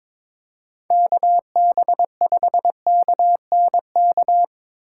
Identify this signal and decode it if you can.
Morse code, heard as a keyed tone, about 22 wpm, 700 Hz. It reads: KB5KNK
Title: USA Call Signs - 6 Characters - 6x Character Spacing - 22wpm